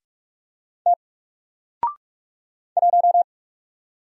Morse code, heard as a keyed tone, about 45 wpm, 700 Hz. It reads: T 1
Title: Single Letter-Number 45wpm